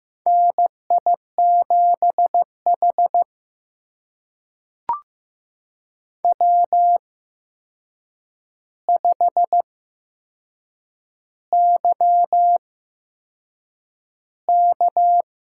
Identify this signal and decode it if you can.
Morse code, heard as a keyed tone, about 15 wpm, spaced as for 6 wpm, 700 Hz. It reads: NI7H W5YK
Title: USA Call Signs - 8x Character Spacing - 15wpm